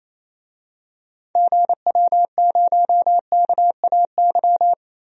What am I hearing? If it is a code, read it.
Morse code, heard as a keyed tone, about 28 wpm, 700 Hz. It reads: GW0KAY
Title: DXCC-100 Call Signs - 6 Characters - 28wpm